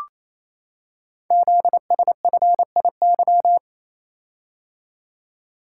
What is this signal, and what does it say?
Morse code, heard as a keyed tone, about 29 wpm, 700 Hz. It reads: ZSFIY
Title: Code Groups - 5 Letters - 28wpm